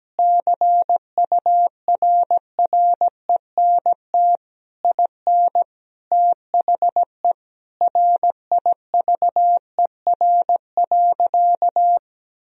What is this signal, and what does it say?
Morse code, heard as a keyed tone, about 17 wpm, 700 Hz. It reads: CURRENT IN THE RIVER.
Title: Sentences from Top 1000 Words - Review Part 4 - 17wpm